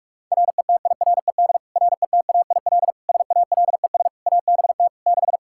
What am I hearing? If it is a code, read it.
Morse code, heard as a keyed tone, about 46 wpm, 700 Hz. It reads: RETIRED RETAIL SALES ABT 6
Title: Practice QSO - Exchange 3 - More Info - 45wpm